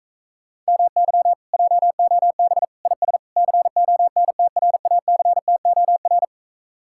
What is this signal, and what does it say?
Morse code, heard as a keyed tone, about 46 wpm, 700 Hz. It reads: MY JOB IS CONTRACTOR